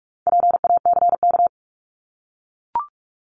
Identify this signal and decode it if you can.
Morse code, heard as a keyed tone, about 44 wpm, 700 Hz. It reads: PACK
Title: Top 1001-1250 Words 45wpm